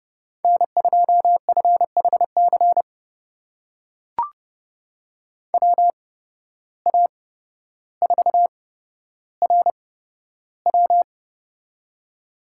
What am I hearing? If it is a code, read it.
Morse code, heard as a keyed tone, about 31 wpm, spaced as for 13 wpm, 700 Hz. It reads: N2FHC WA4RW
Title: USA Call Signs - 5 Characters - 8x Character Spacing - 30wpm